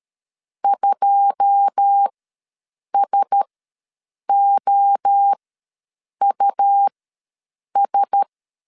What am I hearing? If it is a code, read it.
Morse code, heard as a keyed tone, about 13 wpm, spaced as for 7 wpm, 800 Hz. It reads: 2SOUS
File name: LEZIONI / 9 28 ws 12wpm 3354